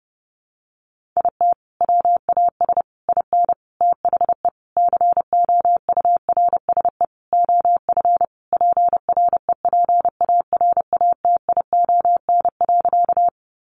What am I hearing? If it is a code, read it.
Morse code, heard as a keyed tone, about 28 wpm, 700 Hz. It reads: IT WAS IN THE COURSE OF PREPARATION.
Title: Sentences from Top 1400 Words 30wpm